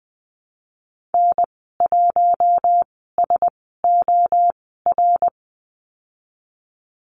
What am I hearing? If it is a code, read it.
Morse code, heard as a keyed tone, about 20 wpm, spaced as for 15 wpm, 700 Hz. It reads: N1SOR
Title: 088 Lesson 15g - Number 5 - Call Signs 20wpm